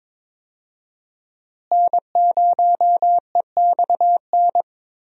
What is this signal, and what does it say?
Morse code, heard as a keyed tone, about 22 wpm, 700 Hz. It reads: N0EXN
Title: USA Call Signs - 5 Characters - Repeated 2x - 22wpm